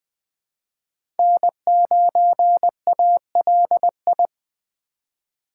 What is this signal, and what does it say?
Morse code, heard as a keyed tone, about 20 wpm, 700 Hz. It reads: N9ALI
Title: USA Call Signs - 5 Characters - 2x Character Spacing - 20wpm